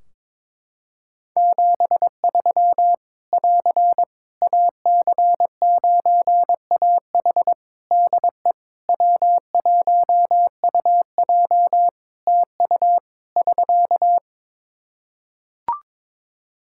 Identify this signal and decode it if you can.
Morse code, heard as a keyed tone, about 22 wpm, 700 Hz. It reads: 73 <AR> AC9AH DE W1UJ TU <SK>
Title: K1USN SST QSO Practice 22wpm